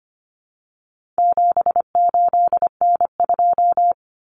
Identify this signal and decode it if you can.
Morse code, heard as a keyed tone, about 25 wpm, 700 Hz. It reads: 78N2
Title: Common Cut Numbers - 4-Digit 25wpm